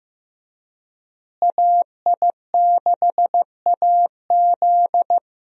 Morse code, heard as a keyed tone, about 15 wpm, 700 Hz. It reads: AI6AZ
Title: USA Call Signs - 5 Characters - Repeated 2x - 15wpm